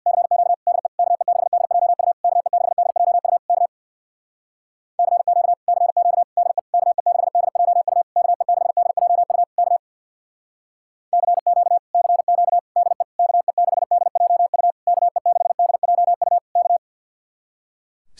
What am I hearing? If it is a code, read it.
Morse code, heard as a keyed tone, about 66 wpm, 700 Hz. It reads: CQ DE KE6DJU KE6DJU K CQ CQ DE KE6DJU KE6DJU K CQ CQ DE KE6DJU KE6DJU K